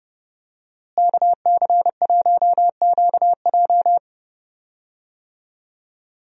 Morse code, heard as a keyed tone, about 32 wpm, 700 Hz. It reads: KC1QJ
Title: USA Call Signs - 5 Characters - Repeated 2x - 30wpm